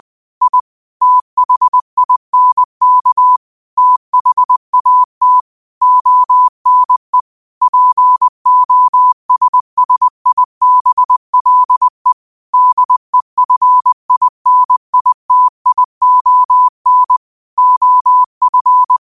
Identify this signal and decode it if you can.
Morse code, heard as a keyed tone, about 20 wpm, 1000 Hz. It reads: I THINK THAT ONE POSSIBLE DEFINITION OF
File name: qotdfiles / 20wpm